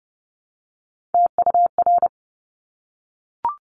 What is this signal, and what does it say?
Morse code, heard as a keyed tone, about 30 wpm, 700 Hz. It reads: TUR
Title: Top Three Letter Combinations in Words 30wpm